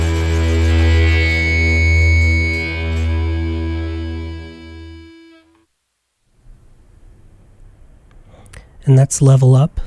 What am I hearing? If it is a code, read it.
Dial-up modem handshake, with speech.